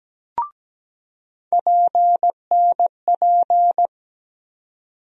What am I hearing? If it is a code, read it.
Morse code, heard as a keyed tone, about 17 wpm, 700 Hz. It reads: PNP
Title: USA Call Sign Postfixes - 3 Letters 17wpm